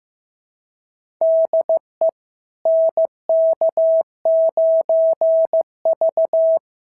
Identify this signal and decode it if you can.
Morse code, heard as a keyed tone, about 14 wpm, 650 Hz. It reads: DE NK9V
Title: SOTA QSO Practice 15wpm